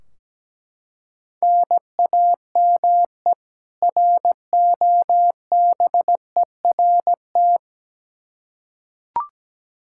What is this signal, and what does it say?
Morse code, heard as a keyed tone, about 17 wpm, 700 Hz. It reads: NAME ROBERT
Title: Practice QSO - Exchange 1 - Beginning - 17wpm